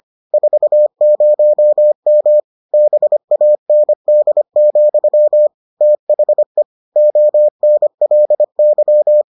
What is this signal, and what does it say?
Morse code, heard as a keyed tone, about 25 wpm, 600 Hz. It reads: 40M BAND, THE ONLY